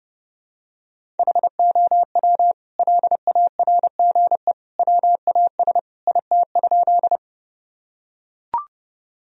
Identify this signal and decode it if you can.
Morse code, heard as a keyed tone, about 30 wpm, 700 Hz. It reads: HOW LARGE WAS IT?
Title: Sentences from Top 1000 Words - Review Part 1 - 30wpm